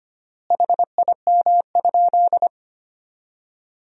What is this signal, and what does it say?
Morse code, heard as a keyed tone, about 25 wpm, 700 Hz. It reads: HIM?
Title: 3x Word-Spacing - Sentences from Top 100 Words - 25wpm